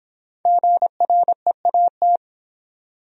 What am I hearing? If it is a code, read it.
Morse code, heard as a keyed tone, about 25 wpm, 700 Hz. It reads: GREAT